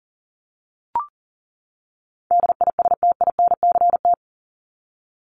Morse code, heard as a keyed tone, about 40 wpm, 700 Hz. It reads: DISTINCT
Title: Top 1501-1750 Words 40wpm